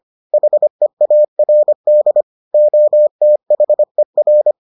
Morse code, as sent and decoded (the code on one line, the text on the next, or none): .... .- .-. -.. / --- - .... .-.
HEARD OTHER